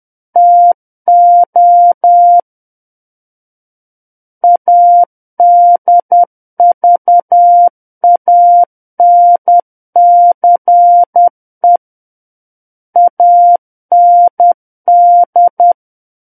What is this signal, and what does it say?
Morse code, heard as a keyed tone, about 10 wpm, 700 Hz. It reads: TO ADVANCE AND
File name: Book One / 01-Chapter 1 - 10 WPM